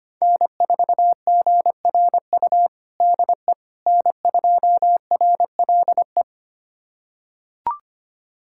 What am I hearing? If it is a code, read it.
Morse code, heard as a keyed tone, about 25 wpm, 700 Hz. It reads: N4GRU DE N2RLE
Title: USA Callsign DE Callsign - 5 Characters - 25wpm